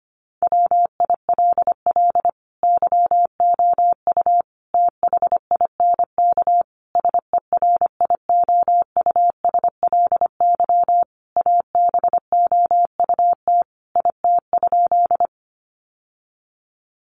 Morse code, heard as a keyed tone, about 25 wpm, 700 Hz. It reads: WILL YOU THINK SERIOUSLY ABOUT IT?
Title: Sentences from Top 1200 Words 25wpm